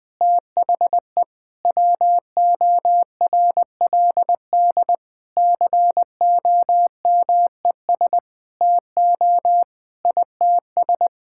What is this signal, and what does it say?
Morse code, heard as a keyed tone, about 20 wpm, 700 Hz. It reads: THE WORLD COMES TO ITS
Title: Sets of 9 Words - 20wpm